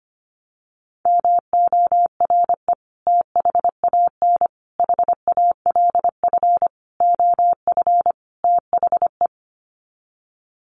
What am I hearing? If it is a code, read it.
Morse code, heard as a keyed tone, about 25 wpm, 700 Hz. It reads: MORE THAN HALF OF THE
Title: Sets of 5 Words 25wpm